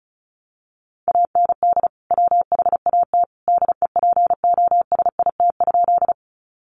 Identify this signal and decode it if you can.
Morse code, heard as a keyed tone, about 34 wpm, 700 Hz. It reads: AND WHAT DEPOSIT?
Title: Sentences from Top 1300 Words 35wpm